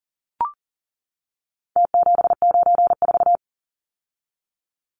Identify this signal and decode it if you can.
Morse code, heard as a keyed tone, about 40 wpm, 700 Hz. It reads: T794
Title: Numbers - Contest Style - 4-Digit 40wpm